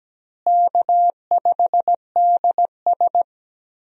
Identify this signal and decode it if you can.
Morse code, heard as a keyed tone, about 17 wpm, 700 Hz. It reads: K5DS
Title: USA Call Signs - 4 Characters - Repeated 3x - 17wpm